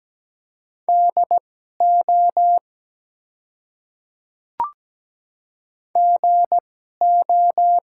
Morse code, heard as a keyed tone, about 17 wpm, spaced as for 12 wpm, 700 Hz. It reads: DO GO
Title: 2x Character Spacing - Top 100 Words - 17wpm